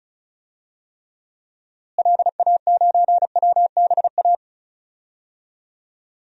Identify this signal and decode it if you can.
Morse code, heard as a keyed tone, about 36 wpm, 700 Hz. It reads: LA9WBA
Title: DXCC-100 Call Signs - 6 Characters - Repeated 3x - 35wpm